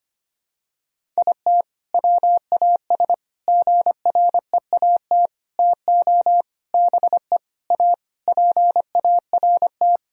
Morse code, heard as a keyed tone, about 25 wpm, 700 Hz. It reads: IT WAS GREAT TO BE A PART
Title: Sentences from Top 1000 Words - Review Part 2 - 25wpm